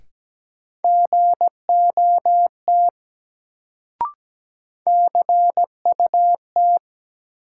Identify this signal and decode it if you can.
Morse code, heard as a keyed tone, about 17 wpm, 700 Hz. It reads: GOT CUT
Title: Sentences from Top 300 Words - New Word Drill - 17wpm